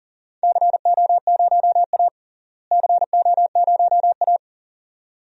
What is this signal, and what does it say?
Morse code, heard as a keyed tone, about 40 wpm, 700 Hz. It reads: CO0A CO0A
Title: DXCC-100 Call Signs - 4 Characters - Repeated 2x - 40wpm